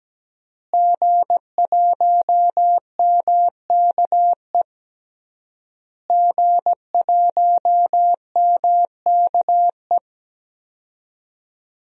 Morse code, heard as a keyed tone, about 17 wpm, 700 Hz. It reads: G1MKE G1MKE
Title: DXCC-100 Call Signs - 5 Characters - Repeated 3x - 17wpm